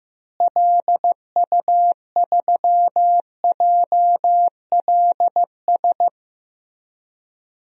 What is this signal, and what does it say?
Morse code, heard as a keyed tone, about 15 wpm, 700 Hz. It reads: LU3JLS
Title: DXCC-100 Call Signs - 6 Characters - Repeated 3x - 15wpm